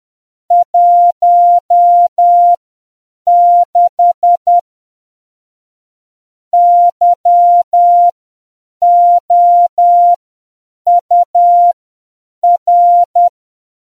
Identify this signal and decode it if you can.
Morse code, heard as a keyed tone, about 10 wpm, spaced as for 7 wpm, 700 Hz. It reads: 16 YOUR